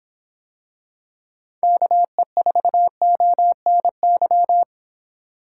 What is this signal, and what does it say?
Morse code, heard as a keyed tone, about 25 wpm, 700 Hz. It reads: KE4ONY